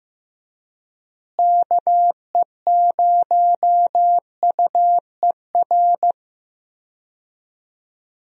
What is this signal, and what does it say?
Morse code, heard as a keyed tone, about 15 wpm, 700 Hz. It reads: KE0UER
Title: USA Call Signs - 6 Characters - Repeated 3x - 15wpm